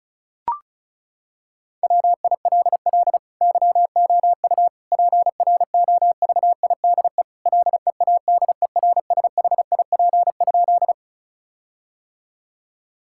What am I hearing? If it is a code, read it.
Morse code, heard as a keyed tone, about 36 wpm, 700 Hz. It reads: WILL YOU PROVIDE LEADERSHIP?
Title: Sentences from Top 1600 Words 35wpm